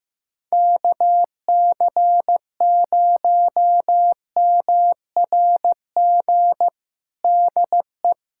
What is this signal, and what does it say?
Morse code, heard as a keyed tone, about 16 wpm, 700 Hz. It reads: KC0MRG DE
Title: USA Callsign DE Callsign - 6 Characters - 15wpm